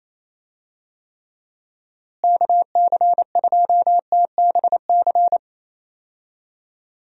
Morse code, heard as a keyed tone, about 28 wpm, 700 Hz. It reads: KC2TBC
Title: USA Call Signs - 6 Characters - Repeated 3x - 28wpm